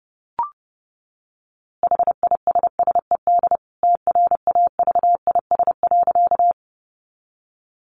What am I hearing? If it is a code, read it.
Morse code, heard as a keyed tone, about 30 wpm, 700 Hz. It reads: HISSED TRAVIS.
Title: A Sound Of Thunder 30wpm